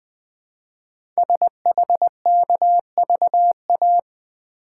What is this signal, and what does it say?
Morse code, heard as a keyed tone, about 20 wpm, 700 Hz. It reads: SHKVA
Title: Code Groups - 5 Letters - 20wpm